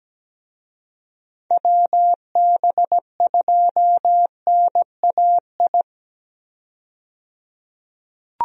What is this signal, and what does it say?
Morse code, heard as a keyed tone, about 17 wpm, 700 Hz. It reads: WB2NAI
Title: USA Call Signs - 6 Characters - 17wpm